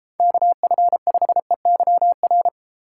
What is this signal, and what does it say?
Morse code, heard as a keyed tone, about 32 wpm, 700 Hz. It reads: KF5EYR